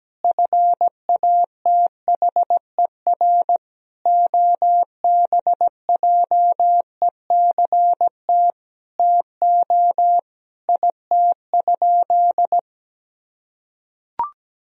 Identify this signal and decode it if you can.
Morse code, heard as a keyed tone, about 17 wpm, 700 Hz. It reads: FATHER OBJECT TO IT?
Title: Sentences from Top 1000 Words - Review Part 3 - 17wpm